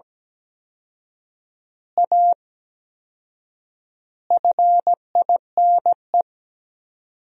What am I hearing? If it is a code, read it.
Morse code, heard as a keyed tone, about 17 wpm, 700 Hz. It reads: A FINE
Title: 4x Word-Spacing - Sentences from Top 500 Words - 17wpm